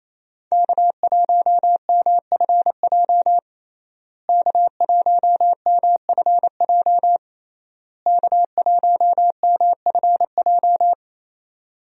Morse code, heard as a keyed tone, about 28 wpm, 700 Hz. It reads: K1MFJ K1MFJ K1MFJ
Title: USA Call Signs - 5 Characters - Repeated 3x - 28wpm